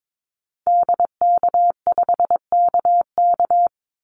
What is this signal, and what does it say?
Morse code, heard as a keyed tone, about 22 wpm, 700 Hz. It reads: DK5KK
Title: Call Signs Repeated 2x - 22wpm